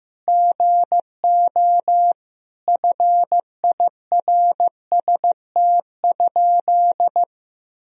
Morse code, heard as a keyed tone, about 15 wpm, 700 Hz. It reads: GO FIRST?
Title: Sentences from Top 1000 Words - Review Part 3 - 15wpm